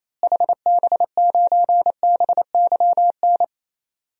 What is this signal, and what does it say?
Morse code, heard as a keyed tone, about 28 wpm, 700 Hz. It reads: HB9BYN
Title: DXCC-100 Call Signs - 6 Characters - Repeated 2x - 28wpm